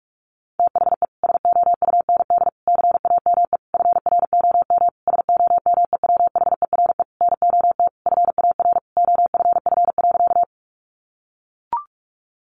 Morse code, heard as a keyed tone, about 44 wpm, 700 Hz. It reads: THE SOUND CAME FROM SOMEWHERE NOT FAR OFF.
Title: Sentences from Top 1400 Words 45wpm